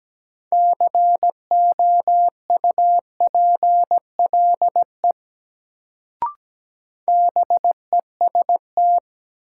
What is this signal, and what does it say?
Morse code, heard as a keyed tone, about 17 wpm, 700 Hz. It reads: COUPLE BEST
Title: Sets of 2 Words - New Word Drill - 17wpm